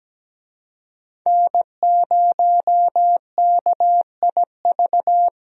Morse code, heard as a keyed tone, about 17 wpm, 700 Hz. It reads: N0KIV
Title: USA Call Signs - 5 Characters - 17wpm